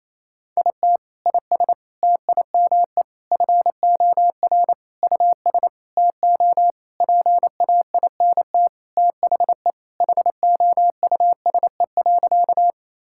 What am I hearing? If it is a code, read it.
Morse code, heard as a keyed tone, about 28 wpm, 700 Hz. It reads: IT IS TIME FOR US TO PAINT THE HOUSE.
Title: Sentences from Top 1000 Words - Review Part 6 - 28wpm